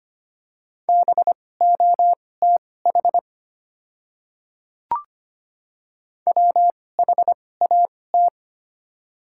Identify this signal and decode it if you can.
Morse code, heard as a keyed tone, about 25 wpm, spaced as for 18 wpm, 700 Hz. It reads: BOTH WHAT
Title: 2x Character Spacing - Top 500 Words - 25wpm